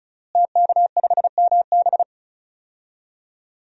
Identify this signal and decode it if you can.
Morse code, heard as a keyed tone, about 36 wpm, 700 Hz. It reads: TK5MB
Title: DXCC-100 Call Signs - 5 Characters - Repeated 3x - 35wpm